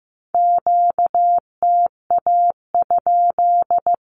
Morse code, heard as a keyed tone, about 15 wpm, 700 Hz. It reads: QTA?
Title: Q-Codes 15wpm